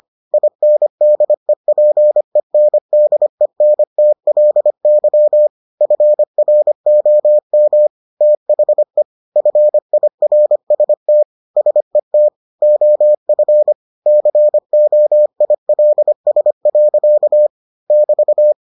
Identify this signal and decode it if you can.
Morse code, heard as a keyed tone, about 25 wpm, 600 Hz. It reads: INDEPENDENTLY FROM THE FIRST SET OF COILS. <BT>